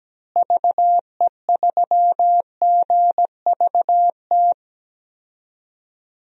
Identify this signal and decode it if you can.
Morse code, heard as a keyed tone, about 17 wpm, 700 Hz. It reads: VE3GVT
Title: DXCC-100 Call Signs - 6 Characters - Repeated 3x - 17wpm